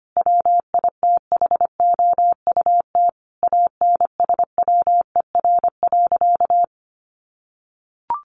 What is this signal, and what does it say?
Morse code, heard as a keyed tone, about 25 wpm, 700 Hz. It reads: WITHOUT ANSWER.
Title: Sentences from Top 1500 Words 25wpm